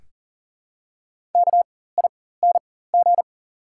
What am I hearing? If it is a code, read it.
Morse code, heard as a keyed tone, about 40 wpm, spaced as for 19 wpm, 690 Hz. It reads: KING
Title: 4x Character Spacing - Top 500 Words - 40wpm